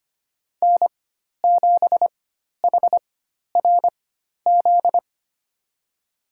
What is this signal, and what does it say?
Morse code, heard as a keyed tone, about 25 wpm, spaced as for 12 wpm, 700 Hz. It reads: N7HRZ
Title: USA Call Signs - 5 Characters - 4x Character Spacing - 25wpm